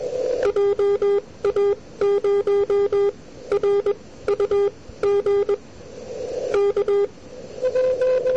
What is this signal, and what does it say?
Morse code, heard as a keyed tone, about 21 wpm, spaced as for 16 wpm, 400 Hz. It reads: JA0RUG K
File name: n6tr / VY2ZMLongpath2008Nov23-2100UTCatJA8ISU